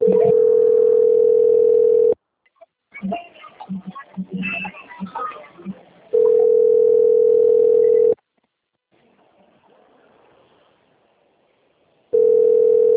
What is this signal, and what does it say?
Ringback tone (North America), 440 + 480 Hz, with speech.
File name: out- - - - - .28577